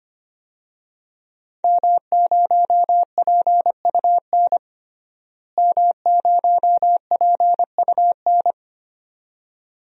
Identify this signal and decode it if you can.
Morse code, heard as a keyed tone, about 25 wpm, 700 Hz. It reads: M0PUN M0PUN
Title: DXCC-100 Call Signs - 5 Characters - Repeated 2x - 25wpm